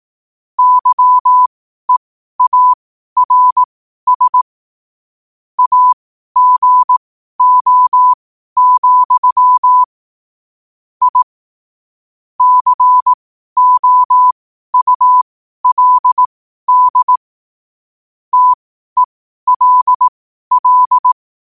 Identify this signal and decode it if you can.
Morse code, heard as a keyed tone, about 18 wpm, spaced as for 12 wpm, 1000 Hz. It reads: YEARS AGO, I COULD TELL
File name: qotdfiles / 12wpm